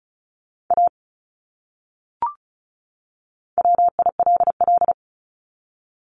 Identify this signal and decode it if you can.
Morse code, heard as a keyed tone, about 34 wpm, 700 Hz. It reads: A WILL
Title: Top 200 Words 35wpm